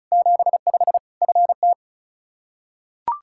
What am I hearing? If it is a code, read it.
Morse code, heard as a keyed tone, about 34 wpm, 700 Hz. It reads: 75 FT
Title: Practice QSO - Exchange 2 - Info - 35wpm